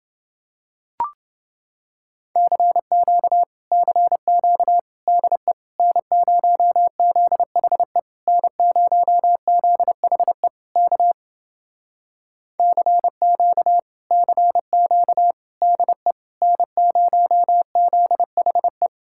Morse code, heard as a keyed tone, about 28 wpm, 700 Hz. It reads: CQ CQ DE N0ZHE N0ZHE K CQ CQ DE N0ZHE